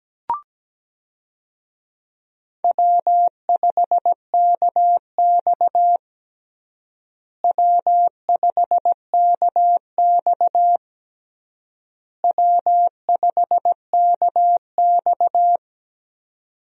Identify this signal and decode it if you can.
Morse code, heard as a keyed tone, about 17 wpm, 700 Hz. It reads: W5KX W5KX W5KX
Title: USA Call Signs - 4 Characters - Repeated 3x - 17wpm